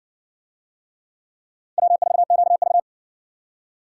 Morse code, heard as a keyed tone, about 52 wpm, 700 Hz. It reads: W4QV